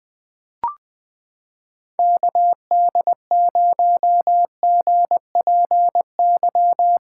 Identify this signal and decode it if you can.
Morse code, heard as a keyed tone, about 20 wpm, 700 Hz. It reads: KD0GPY
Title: USA Callsign DE Callsign 20wpm